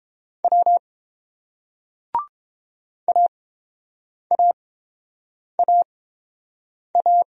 Morse code, heard as a keyed tone, about 30 wpm, spaced as for 7 wpm, 700 Hz. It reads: W AAAA